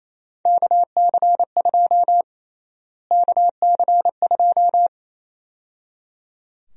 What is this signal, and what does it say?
Morse code, heard as a keyed tone, about 28 wpm, 700 Hz. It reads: KC2 KC2
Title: USA Call Sign Prefixes - Repeated 2x - 28wpm